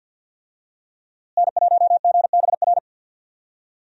Morse code, heard as a keyed tone, about 51 wpm, 700 Hz. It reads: N1GBR
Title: USA Call Signs - 5 Characters - Repeated 3x - 50wpm